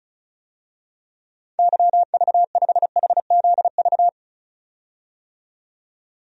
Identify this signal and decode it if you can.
Morse code, heard as a keyed tone, about 34 wpm, 700 Hz. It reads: YV5HZV
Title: DXCC-100 Call Signs - 6 Characters - 35wpm